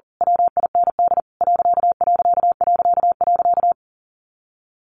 Morse code, heard as a keyed tone, about 40 wpm, 700 Hz. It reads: WIND ....
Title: A Sound Of Thunder 40wpm